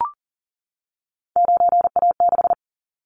Morse code, heard as a keyed tone, about 40 wpm, 700 Hz. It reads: 9A6
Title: Callsign Prefixes for Top 100 DXCC - 40wpm